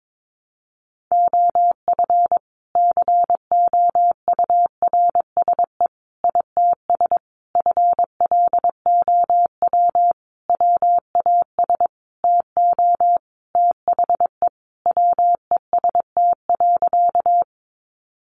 Morse code, heard as a keyed tone, about 22 wpm, 700 Hz. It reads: OF COURSE ITS FLOW WAS TO THE WEST.
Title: Sentences from Top 1200 Words 22wpm